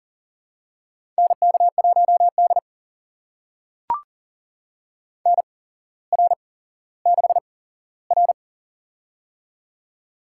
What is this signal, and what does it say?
Morse code, heard as a keyed tone, about 40 wpm, spaced as for 17 wpm, 700 Hz. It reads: NK1D NR6R
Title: USA Call Signs - 8x Character Spacing - 40wpm